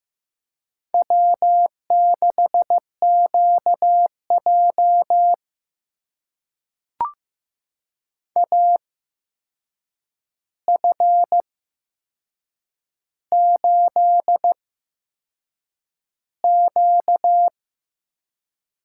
Morse code, heard as a keyed tone, about 15 wpm, spaced as for 6 wpm, 700 Hz. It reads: W6QJ AF8Q
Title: USA Call Signs - 4 Characters - 8x Character Spacing - 15wpm